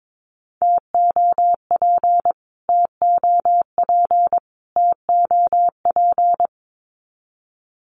Morse code, heard as a keyed tone, about 22 wpm, 700 Hz. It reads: TOP TOP TOP
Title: Top 500 Words Repeated 22wpm